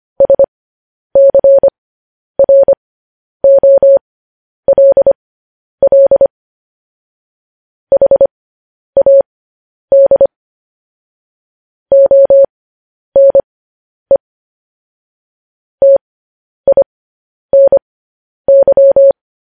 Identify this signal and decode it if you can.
Morse code, heard as a keyed tone, about 25 wpm, spaced as for 10 wpm, 550 Hz. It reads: SCROLL HAD ONE TINY